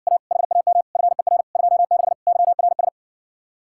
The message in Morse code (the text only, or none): A HAM FER 26 YRS